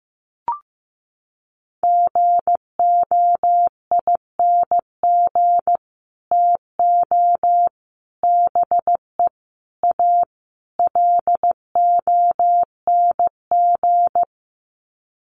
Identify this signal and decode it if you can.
Morse code, heard as a keyed tone, about 15 wpm, 700 Hz. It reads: GOING TO BE A LONG
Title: Sets of 5 Words Encore2 15wpm